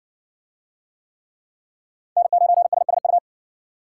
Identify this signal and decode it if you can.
Morse code, heard as a keyed tone, about 54 wpm, 700 Hz. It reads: N9SSU